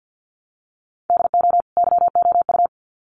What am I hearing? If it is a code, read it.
Morse code, heard as a keyed tone, about 50 wpm, 700 Hz. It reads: DO YOU
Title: Sets of 2 Words 50wpm